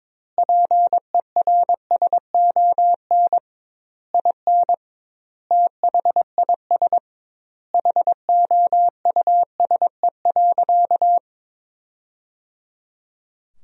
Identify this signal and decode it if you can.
Morse code, heard as a keyed tone, about 23 wpm, 700 Hz. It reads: PERSON IN THIS HOUSE.
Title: 2x Word-Spacing - Sentences from Top 400 Words - 22wpm